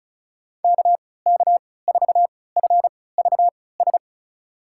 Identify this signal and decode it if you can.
Morse code, heard as a keyed tone, about 36 wpm, spaced as for 20 wpm, 700 Hz. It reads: KK4FVS
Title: USA Call Signs - 6 Characters - 3x Character Spacing - 35wpm